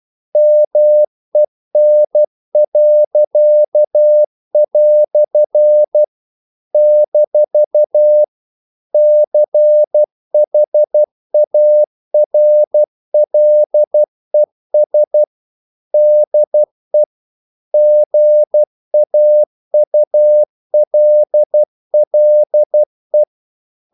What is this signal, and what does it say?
Morse code, heard as a keyed tone, about 12 wpm, 600 Hz. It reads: MEN." - CHARLES DE GAULLE
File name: CW / Practice / EnglishQuotations0312